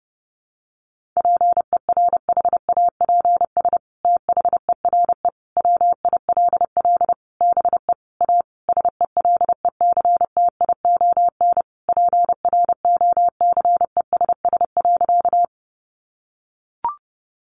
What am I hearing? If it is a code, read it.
Morse code, heard as a keyed tone, about 30 wpm, 700 Hz. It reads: PERHAPS THERE WILL BE A SELECTION PROCESS.
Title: Sentences from Top 1400 Words 30wpm